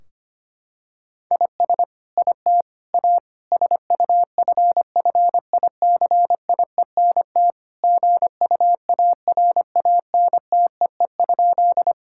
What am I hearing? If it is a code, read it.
Morse code, heard as a keyed tone, about 25 wpm, 700 Hz. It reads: IS IT A SUFFICIENT GUARANTEE?
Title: Sentences from Top 1600 Words 25wpm